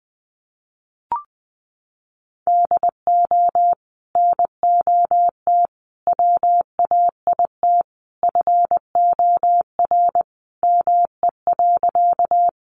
Morse code, heard as a keyed tone, about 19 wpm, 700 Hz. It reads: DO NOT WAIT FOR ME.
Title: Sentences from Top 600 Words 20wpm